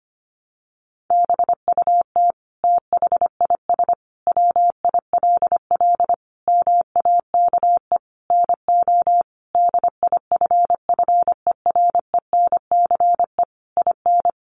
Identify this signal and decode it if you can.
Morse code, heard as a keyed tone, about 25 wpm, 700 Hz. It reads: BUT THIS WILL MAKE NO DIFFERENCE IN
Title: Sentences from Top 800 Words 25wpm